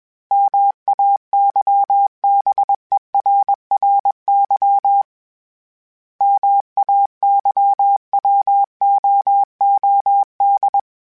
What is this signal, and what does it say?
Morse code, heard as a keyed tone, about 21 wpm, 800 Hz. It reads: MAYBERRY MAYWOOD
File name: fromSMs / FromSMs mp3s / Cities@20